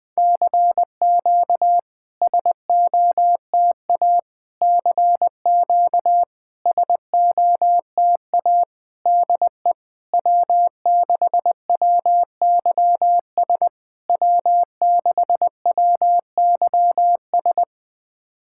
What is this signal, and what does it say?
Morse code, heard as a keyed tone, about 20 wpm, 700 Hz. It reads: CQ SOTA CQ SOTA DE W6WYS W6WYS
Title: SOTA QSO Practice 20wpm